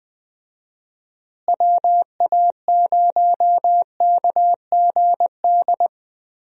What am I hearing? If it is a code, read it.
Morse code, heard as a keyed tone, about 20 wpm, 700 Hz. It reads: WA0KGD
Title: USA Call Signs - 6 Characters - Repeated 2x - 20wpm